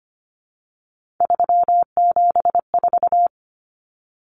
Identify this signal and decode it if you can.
Morse code, heard as a keyed tone, about 25 wpm, 700 Hz. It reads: 374
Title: Numbers - 3-Digit 25wpm